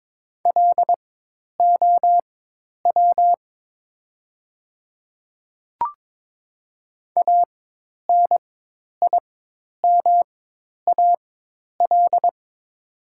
Morse code, heard as a keyed tone, about 22 wpm, spaced as for 10 wpm, 700 Hz. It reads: LOW ANIMAL
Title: 4x Character Spacing - Top 200 Words - 22wpm